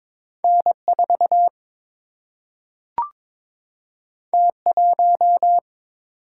Morse code, heard as a keyed tone, about 23 wpm, 700 Hz. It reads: N 4 T 1
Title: Single-Letter Single-Number 22wpm